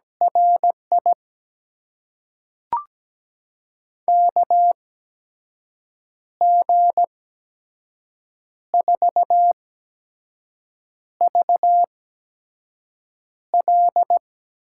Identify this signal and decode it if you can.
Morse code, heard as a keyed tone, about 17 wpm, spaced as for 5 wpm, 700 Hz. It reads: RI KG4VL
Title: USA Call Signs - 6 Characters - 8x Character Spacing - 17wpm